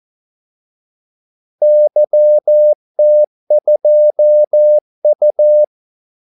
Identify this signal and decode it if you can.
Morse code, heard as a keyed tone, about 14 wpm, 600 Hz. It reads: YT2U
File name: Callsigns-14wpm